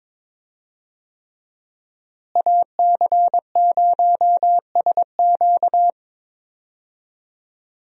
Morse code, heard as a keyed tone, about 22 wpm, 700 Hz. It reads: AC0SQ
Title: USA Call Signs - 5 Characters - Repeated 3x - 22wpm